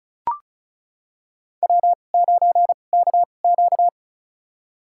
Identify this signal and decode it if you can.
Morse code, heard as a keyed tone, about 36 wpm, spaced as for 25 wpm, 700 Hz. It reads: W9KQ
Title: USA Call Signs - 4 Characters - 2x Character Spacing - 35wpm